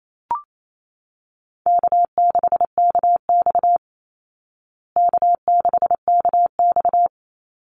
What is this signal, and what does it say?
Morse code, heard as a keyed tone, about 28 wpm, 700 Hz. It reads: K6KX K6KX
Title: Easy Call Signs Repeated 3x - 28wpm